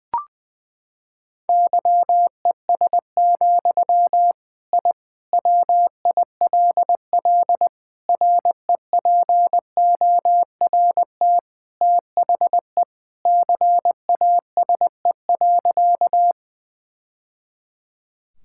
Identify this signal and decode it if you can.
Morse code, heard as a keyed tone, about 19 wpm, 700 Hz. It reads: YES, I WILL REPORT THE CASE.
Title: Sentences from Top 1000 Words - Review Part 6 - 20wpm